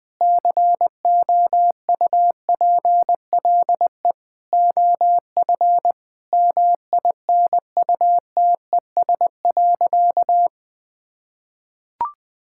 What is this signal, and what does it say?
Morse code, heard as a keyed tone, about 20 wpm, 700 Hz. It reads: COUPLE OF MINUTES.
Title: Sentences with 3-Word Phrases - 20wpm